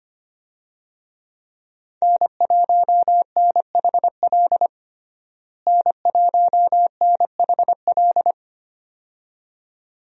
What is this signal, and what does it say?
Morse code, heard as a keyed tone, about 25 wpm, 700 Hz. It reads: N1NHL N1NHL
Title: USA Call Signs - 5 Characters - Repeated 2x - 25wpm